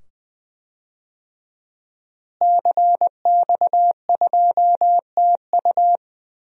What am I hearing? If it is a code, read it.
Morse code, heard as a keyed tone, about 20 wpm, 700 Hz. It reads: CX2TU
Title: DXCC-100 Call Signs - 5 Characters - Repeated 2x - 20wpm